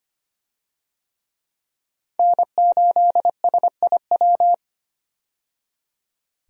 Morse code, heard as a keyed tone, about 25 wpm, 700 Hz. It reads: N8SIW
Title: USA Call Signs - 5 Characters - 25wpm